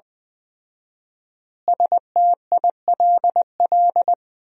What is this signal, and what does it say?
Morse code, heard as a keyed tone, about 20 wpm, 700 Hz. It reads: STILL
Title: 4x Word-Spacing - Sentences from Top 400 Words - 20wpm